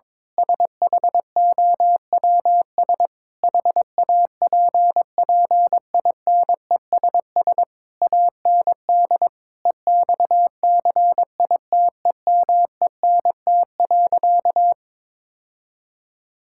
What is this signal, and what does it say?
Morse code, heard as a keyed tone, about 22 wpm, 700 Hz. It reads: SHOWS HAPPINESS AND EXCITEMENT.
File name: Difficult Sentences with Hints - 22wpm